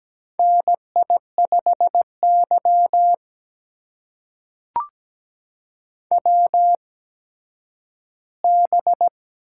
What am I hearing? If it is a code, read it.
Morse code, heard as a keyed tone, about 17 wpm, spaced as for 10 wpm, 700 Hz. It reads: NI5Y WB
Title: USA Call Signs - 8x Character Spacing - 17wpm